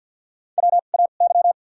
Morse code, heard as a keyed tone, about 52 wpm, spaced as for 36 wpm, 700 Hz. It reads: WAY
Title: 2x Character Spacing - Top 100 Words - 50wpm